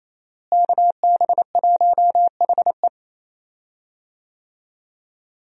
Morse code, heard as a keyed tone, about 28 wpm, 700 Hz. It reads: KB1HE
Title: USA Call Signs - 5 Characters - Repeated 3x - 28wpm